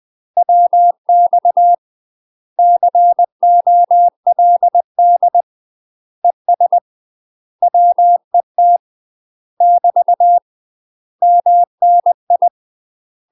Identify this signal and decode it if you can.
Morse code, heard as a keyed tone, about 20 wpm, 700 Hz. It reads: WX COLD ES WET <BT> MNI
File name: UK MASTER QSOs TRAINING/20WPM